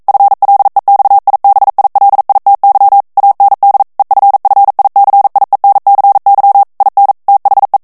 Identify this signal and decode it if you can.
Morse code, heard as a keyed tone, about 42 wpm, 800 Hz. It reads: FLEXIBILITY AND EFFICIENCY IN THE